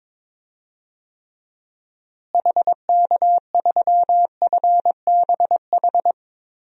Morse code, heard as a keyed tone, about 22 wpm, 700 Hz. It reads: HK3FBH
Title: DXCC-100 Call Signs - 6 Characters - Repeated 2x - 22wpm